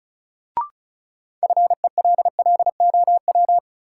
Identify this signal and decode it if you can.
Morse code, heard as a keyed tone, about 36 wpm, 700 Hz. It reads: FELLOW
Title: Sets of 5 Words Encore - New Word Drill - 35wpm